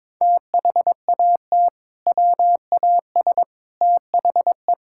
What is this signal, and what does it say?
Morse code, heard as a keyed tone, about 22 wpm, 700 Hz. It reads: THAT WAS THE
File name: practice / 22wpm / Sentences with 3-Word Phrases - 22wpm